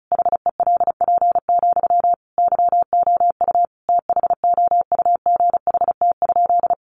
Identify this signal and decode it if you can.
Morse code, heard as a keyed tone, about 34 wpm, 700 Hz. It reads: HELP, YOU THOUGHT?
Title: Sentences from Top 300 Words 35wpm